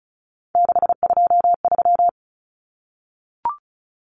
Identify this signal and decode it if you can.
Morse code, heard as a keyed tone, about 34 wpm, 700 Hz. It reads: 623
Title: Common Cut Numbers - 3-Digit 35wpm